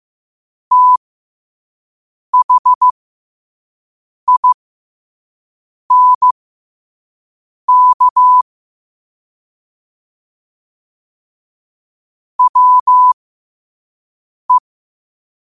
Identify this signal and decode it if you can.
Morse code, heard as a keyed tone, about 15 wpm, spaced as for 5 wpm, 1000 Hz. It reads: THINK WE
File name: qotdfiles / 5wpm